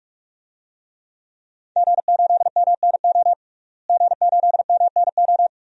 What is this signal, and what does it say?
Morse code, heard as a keyed tone, about 42 wpm, 700 Hz. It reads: G8MNO G8MNO
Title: DXCC-100 Call Signs - 5 Characters - Repeated 2x - 45wpm